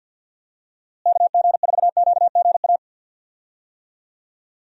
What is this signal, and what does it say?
Morse code, heard as a keyed tone, about 53 wpm, 700 Hz. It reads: KG4QGA
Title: USA Call Signs - 6 Characters - Repeated 3x - 50wpm